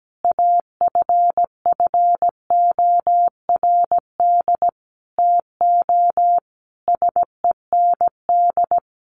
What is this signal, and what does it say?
Morse code, heard as a keyed tone, about 17 wpm, 700 Hz. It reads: AFFORD TO SEND
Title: Sentences from Top 1500 Words 17wpm